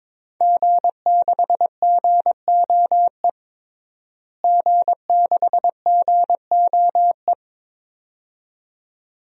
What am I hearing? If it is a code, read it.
Morse code, heard as a keyed tone, about 22 wpm, 700 Hz. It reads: G6GOE G6GOE
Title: DXCC-100 Call Signs - 5 Characters - Repeated 2x - 22wpm